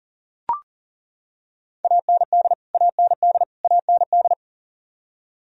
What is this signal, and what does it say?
Morse code, heard as a keyed tone, about 37 wpm, 700 Hz. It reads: AND AND AND
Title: Top 100 Words Repeated 40wpm